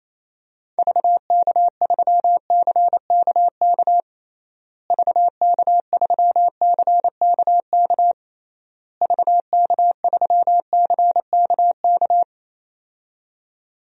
Morse code, heard as a keyed tone, about 28 wpm, 700 Hz. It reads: VK3CKK VK3CKK VK3CKK
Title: DXCC-100 Call Signs - 6 Characters - Repeated 3x - 28wpm